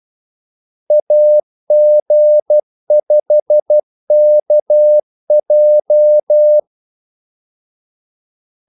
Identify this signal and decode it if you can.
Morse code, heard as a keyed tone, about 12 wpm, 600 Hz. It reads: AG5KJ
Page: 2 ekstra ordmellemrum mellem kaldesignalerne